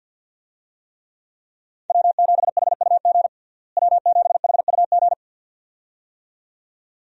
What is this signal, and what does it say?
Morse code, heard as a keyed tone, about 53 wpm, 700 Hz. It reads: W7HUG W7HUG
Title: USA Call Signs - 5 Characters - Repeated 2x - 50wpm